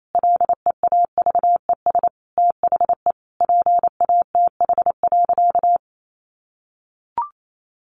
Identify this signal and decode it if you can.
Morse code, heard as a keyed tone, about 28 wpm, 700 Hz. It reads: LEAVES THE PATH.
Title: A Sound Of Thunder 28wpm